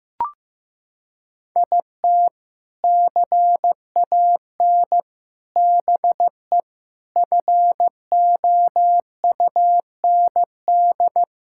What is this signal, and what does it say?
Morse code, heard as a keyed tone, about 16 wpm, 700 Hz. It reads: IT CAN BE FOUND
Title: Sentences from Top 1000 Words - Review Part 5 - 15wpm